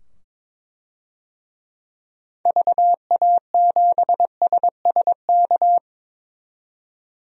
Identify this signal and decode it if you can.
Morse code, heard as a keyed tone, about 21 wpm, 700 Hz. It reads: VA7SSK
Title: DXCC-100 Call Signs - 6 Characters - 22wpm